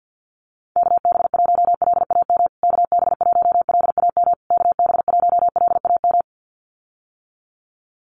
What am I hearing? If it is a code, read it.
Morse code, heard as a keyed tone, about 50 wpm, 700 Hz. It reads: KB1LAM KB1LAM KB1LAM
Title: USA Call Signs Repeated 3x 50wpm